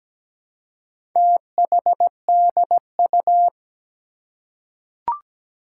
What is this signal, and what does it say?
Morse code, heard as a keyed tone, about 17 wpm, 700 Hz. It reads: THDU
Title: Four Challenging Letter-Number Combinations 17wpm